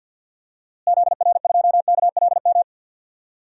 Morse code, heard as a keyed tone, about 46 wpm, 700 Hz. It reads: ZR2KLM
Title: DXCC-100 Call Signs - 6 Characters - Repeated 2x - 50wpm